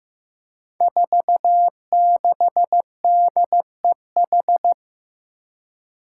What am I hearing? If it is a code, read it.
Morse code, heard as a keyed tone, about 15 wpm, 700 Hz. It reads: 46DEH
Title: Five Challenging Letter-Number Combinations 15wpm